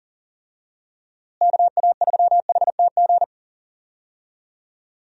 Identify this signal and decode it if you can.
Morse code, heard as a keyed tone, about 40 wpm, 700 Hz. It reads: KA3HTG
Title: USA Call Signs - 6 Characters - Repeated 3x - 40wpm